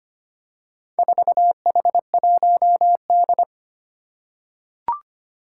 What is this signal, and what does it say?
Morse code, heard as a keyed tone, about 25 wpm, 700 Hz. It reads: 4H1D
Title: Four Challenging Letter-Number Combinations 25wpm